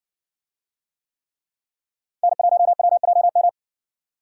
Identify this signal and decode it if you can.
Morse code, heard as a keyed tone, about 57 wpm, 700 Hz. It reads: N1WJM